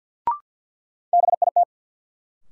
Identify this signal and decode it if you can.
Morse code, heard as a keyed tone, about 53 wpm, 700 Hz. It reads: BIT
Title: Sets of 3 Words - New Word Drill - 50wpm